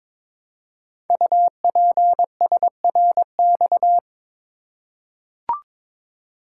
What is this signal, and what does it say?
Morse code, heard as a keyed tone, about 22 wpm, 700 Hz. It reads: UPSRX
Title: Code Groups - 5 Letters - 22wpm